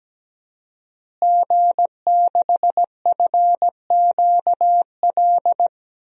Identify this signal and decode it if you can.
Morse code, heard as a keyed tone, about 17 wpm, 700 Hz. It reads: G6FQL
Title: DXCC-100 Call Signs - 5 Characters - Repeated 2x - 17wpm